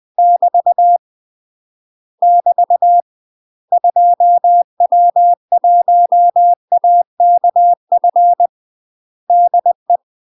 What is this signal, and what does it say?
Morse code, heard as a keyed tone, about 20 wpm, 700 Hz. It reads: <BT> <BT> 2W1AKF DE
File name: UK MASTER QSOs TRAINING / 20WPM / UK MASTER QSOs 0070